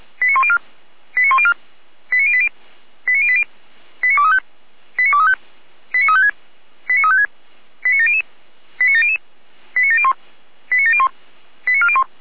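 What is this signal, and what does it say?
Five-tone selective call (ZVEI), heard as quick runs of five tones, tones of about 70 ms. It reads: 89104 89104 89080 89080 89235 89235 89356 89356 89700 89700 89701 89701 89401
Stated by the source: Received on SDR, ZVEI five-tone code 89304.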